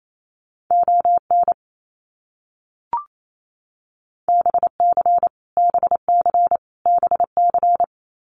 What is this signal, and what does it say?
Morse code, heard as a keyed tone, about 28 wpm, 700 Hz. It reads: ON BC BC BC
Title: Canadian Provinces and Territories - Repeated 3x - 28wpm